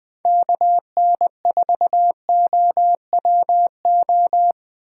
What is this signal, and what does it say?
Morse code, heard as a keyed tone, about 20 wpm, 700 Hz. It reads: KN4OWO